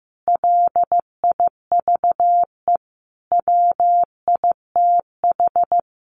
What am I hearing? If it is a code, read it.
Morse code, heard as a keyed tone, about 15 wpm, 700 Hz. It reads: LIVE WITH
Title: Sentences from Top 1100 Words 15wpm